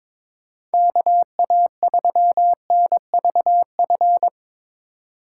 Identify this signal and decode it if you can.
Morse code, heard as a keyed tone, about 22 wpm, 700 Hz. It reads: KA3NVF
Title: USA Call Signs - 6 Characters - 4x Character Spacing - 22wpm